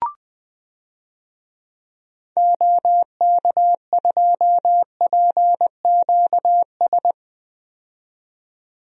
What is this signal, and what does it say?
Morse code, heard as a keyed tone, about 20 wpm, 700 Hz. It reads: OK2PQS
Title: DXCC-100 Call Signs - 6 Characters - 20wpm